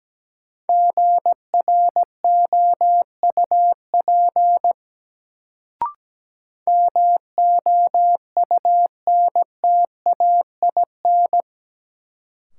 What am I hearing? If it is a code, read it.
Morse code, heard as a keyed tone, about 17 wpm, 700 Hz. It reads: GROUP MOUNTAIN
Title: Sentences from Top 1000 Words - Review Part 1 - New Word Drill - 17wpm